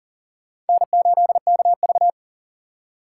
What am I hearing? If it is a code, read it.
Morse code, heard as a keyed tone, about 40 wpm, 700 Hz. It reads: N8KV
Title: USA Call Signs - 4 Characters - 6x Character Spacing - 40wpm